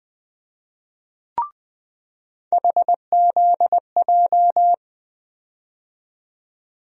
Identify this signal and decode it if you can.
Morse code, heard as a keyed tone, about 20 wpm, 700 Hz. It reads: HZJ
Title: USA Call Sign Postfixes - 3 Letters 20wpm